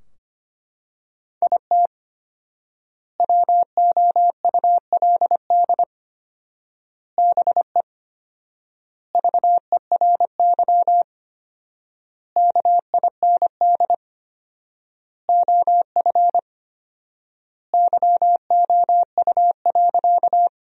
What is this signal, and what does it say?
Morse code, heard as a keyed tone, about 25 wpm, 700 Hz. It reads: IT WOULD BE VERY KIND OF YOU.
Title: 4x Word-Spacing - Sentences from Top 200 Words - 25wpm